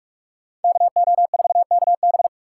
Morse code, heard as a keyed tone, about 42 wpm, 700 Hz. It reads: KO4KB